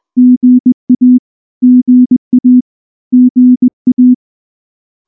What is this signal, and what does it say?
Morse code, heard as a keyed tone, about 19 wpm, 250 Hz. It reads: GA GA GA
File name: morse / morse-training-20WPM-6